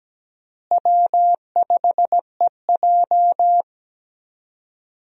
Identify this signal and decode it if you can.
Morse code, heard as a keyed tone, about 17 wpm, 700 Hz. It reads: W5EJ
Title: USA Call Signs - 4 Characters - Repeated 3x - 17wpm